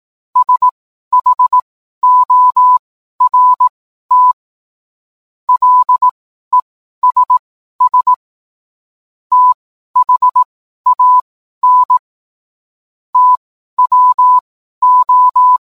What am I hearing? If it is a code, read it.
Morse code, heard as a keyed tone, about 18 wpm, spaced as for 12 wpm, 1000 Hz. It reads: SHORT LESS THAN TWO